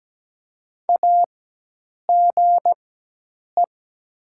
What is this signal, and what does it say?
Morse code, heard as a keyed tone, about 17 wpm, spaced as for 8 wpm, 700 Hz. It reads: AGE
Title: 4x Character Spacing - Top 500 Words - 17wpm